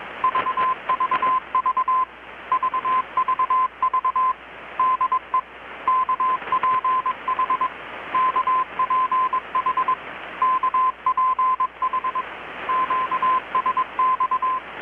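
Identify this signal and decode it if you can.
Morse code, heard as a keyed tone, about 22 wpm, 1030 Hz, with radio noise. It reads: VVV VVV DE KPH KPH KPH QSX
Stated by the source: Morse from KPH's Night of Nights XX maritime radiotelegraphy event, received in Chiapas, Mexico about 3,800 km away on a Kenwood R-600 with a grounded 10 m long-wire antenna.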